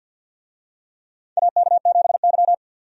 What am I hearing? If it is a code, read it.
Morse code, heard as a keyed tone, about 53 wpm, 700 Hz. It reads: AK7Y
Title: USA Call Signs - 4 Characters - 50wpm